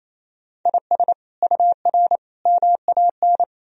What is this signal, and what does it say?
Morse code, heard as a keyed tone, about 28 wpm, 700 Hz. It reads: IS UR MAN
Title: Sentences from Top 400 Words and Abbreviations 28wpm